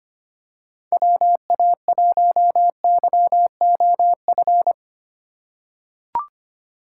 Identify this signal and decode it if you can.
Morse code, heard as a keyed tone, about 25 wpm, 700 Hz. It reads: WA1YOF
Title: USA Call Signs - 6 Characters - 2x Character Spacing - 25wpm